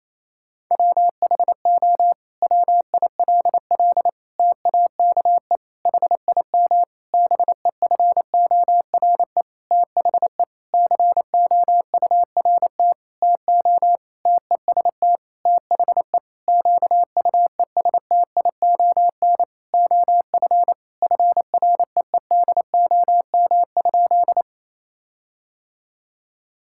Morse code, heard as a keyed tone, about 28 wpm, 700 Hz. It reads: WHO WILL TAKE HIM BEFORE THE COURT TO TEST THE QUESTION OF FREEDOM?
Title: Sentences from Top 1600 Words 28wpm